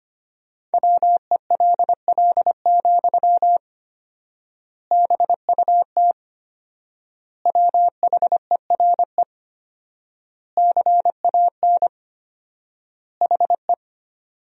Morse code, heard as a keyed tone, about 25 wpm, 700 Hz. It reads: WELL, BUT WHERE CAN HE
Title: 4x Word-Spacing - Sentences from Top 500 Words - 25wpm